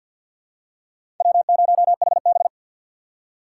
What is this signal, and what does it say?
Morse code, heard as a keyed tone, about 51 wpm, 700 Hz. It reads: W0HB
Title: USA Call Signs - 4 Characters - 3x Character Spacing - 50wpm